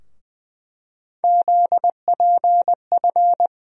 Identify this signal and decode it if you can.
Morse code, heard as a keyed tone, about 20 wpm, 700 Hz. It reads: ZPF
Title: USA Call Sign Postfixes - 3 Letters 20wpm